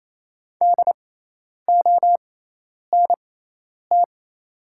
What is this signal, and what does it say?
Morse code, heard as a keyed tone, about 28 wpm, spaced as for 10 wpm, 690 Hz. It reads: DONT
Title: 6x Character Spacing - Top 500 Words - 28wpm